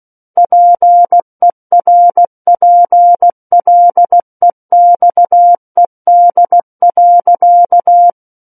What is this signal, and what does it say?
Morse code, heard as a keyed tone, about 16 wpm, 700 Hz. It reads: PERPLEXED.
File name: Book One / 07-Chapter 7 - 16 WPM